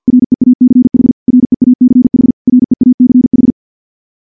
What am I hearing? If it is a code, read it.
Morse code, heard as a keyed tone, about 50 wpm, 250 Hz. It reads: REACH REACH REACH
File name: morse / morse-training-50WPM